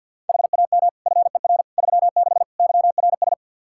Morse code, heard as a keyed tone, about 52 wpm, 700 Hz. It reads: HAM FER 36 YRS
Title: Practice QSO - Exchange 3 - More Info - 50wpm